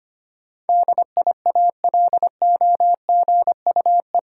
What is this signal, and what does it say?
Morse code, heard as a keyed tone, about 25 wpm, 700 Hz. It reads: DIALOGUE
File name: Eight Letter Words - 25wpm